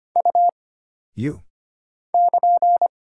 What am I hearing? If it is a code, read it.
Morse code, heard as a keyed tone, about 25 wpm, 700 Hz. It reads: U <KN>
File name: Single Character (Letter:Number:Punctuation:Prosign) - Mind-Melt - 25wpm